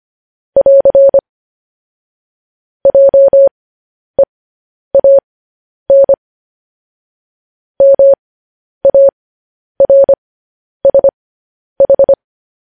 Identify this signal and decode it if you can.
Morse code, heard as a keyed tone, about 25 wpm, spaced as for 10 wpm, 550 Hz. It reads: <AR> JEAN MARSH
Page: news headlines in morse code at 10 wpm